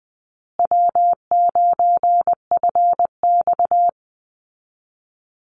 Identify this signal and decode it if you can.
Morse code, heard as a keyed tone, about 20 wpm, 700 Hz. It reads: W9FX
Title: Easy Call Signs Repeated 3x - 20wpm